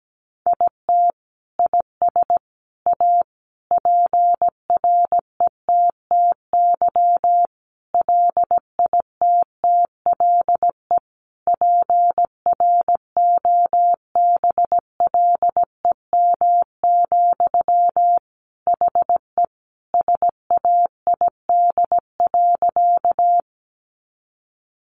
Morse code, heard as a keyed tone, about 17 wpm, 700 Hz. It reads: IT IS A PRETTY LITTLE PROBLEM, HE SAID.
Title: Sentences from Top 1000 Words 17wpm